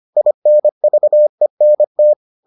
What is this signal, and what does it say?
Morse code, heard as a keyed tone, about 25 wpm, 600 Hz. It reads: INVENT